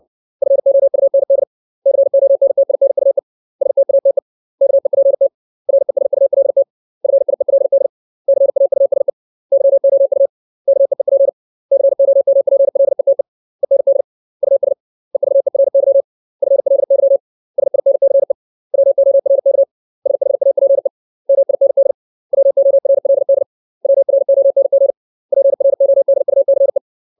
Morse code, heard as a keyed tone, about 60 wpm, 550 Hz. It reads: FOUND CONTINUE STATE KEPT RESULT FIELD CAUSE YOU KEEP COMPLETE END US EVERY FLY SINCE WORK SHAPE MIND WORLD WRONG PRODUCE
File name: 632 words / 632 Words 60WPM